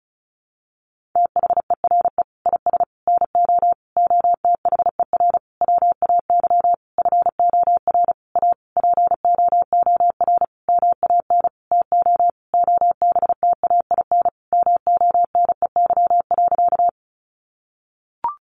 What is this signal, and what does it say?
Morse code, heard as a keyed tone, about 34 wpm, 700 Hz. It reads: THERE IS NO OTHER WAY FOR A POOR MAN TO OBTAIN MONEY.
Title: Sentences from Top 1300 Words 35wpm